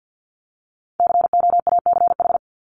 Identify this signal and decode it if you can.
Morse code, heard as a keyed tone, about 50 wpm, 700 Hz. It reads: COACH
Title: Top 1751-2000 Words 50wpm